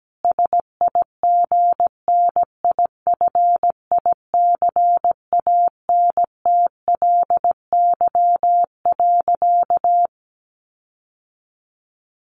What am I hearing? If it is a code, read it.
Morse code, heard as a keyed tone, about 17 wpm, 700 Hz. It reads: SIGNIFICANTLY.
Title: Sentences from Top 1500 Words 17wpm